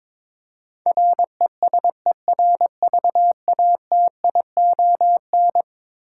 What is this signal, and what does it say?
Morse code, heard as a keyed tone, about 22 wpm, 700 Hz. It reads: RESERVATION
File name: practice / 22wpm / Eleven Letter Words - 22wpm